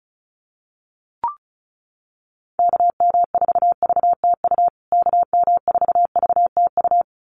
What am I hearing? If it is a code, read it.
Morse code, heard as a keyed tone, about 34 wpm, 700 Hz. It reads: KM4VTU KM4VTU
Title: USA Call Signs Repeated 2x 35wpm